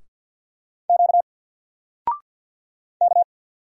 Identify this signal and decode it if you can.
Morse code, heard as a keyed tone, about 50 wpm, 700 Hz. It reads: Q K
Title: Single Letter-Number Alt 50wpm